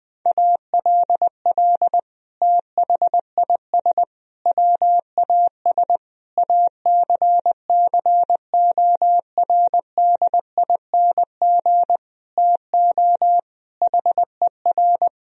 Morse code, heard as a keyed tone, about 21 wpm, 700 Hz. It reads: ALL THIS WAS ACCORDING TO HER
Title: Sentences from Top 1000 Words - Review Part 4 - 20wpm